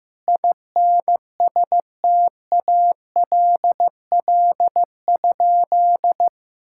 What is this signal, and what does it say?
Morse code, heard as a keyed tone, about 15 wpm, 700 Hz. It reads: INSTALL?
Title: Sentences from Top 1600 Words 15wpm